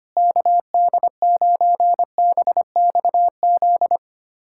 Morse code, heard as a keyed tone, about 25 wpm, 700 Hz. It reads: KD9BXZ
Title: USA Callsign DE Callsign 25wpm